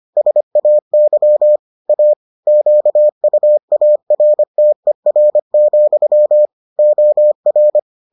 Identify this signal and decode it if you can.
Morse code, heard as a keyed tone, about 25 wpm, 600 Hz. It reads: SAY A QUARTER, OR